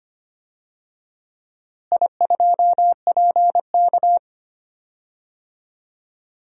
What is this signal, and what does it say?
Morse code, heard as a keyed tone, about 25 wpm, 700 Hz. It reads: I2PK
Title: DXCC-100 Call Signs - 4 Characters - Repeated 3x - 25wpm